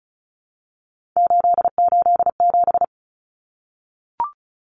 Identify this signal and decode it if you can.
Morse code, heard as a keyed tone, about 34 wpm, 700 Hz. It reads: 887
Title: Common Cut Numbers - 3-Digit 35wpm